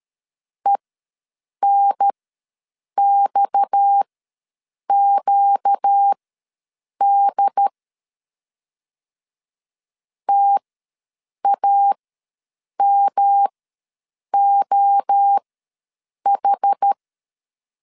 Morse code, heard as a keyed tone, about 13 wpm, spaced as for 7 wpm, 800 Hz. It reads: ENXQD TAMOH
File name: LEZIONI / 6 PKX0 / 6 PKX ws 12wpm 3106